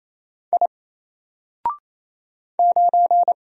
Morse code, heard as a keyed tone, about 27 wpm, 700 Hz. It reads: I 9
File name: Single-Letter-Number-28wpm